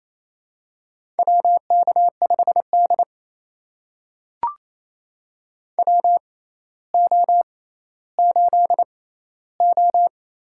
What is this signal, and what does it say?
Morse code, heard as a keyed tone, about 29 wpm, spaced as for 15 wpm, 700 Hz. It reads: WK5D WO8O
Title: USA Call Signs - 4 Characters - 6x Character Spacing - 28wpm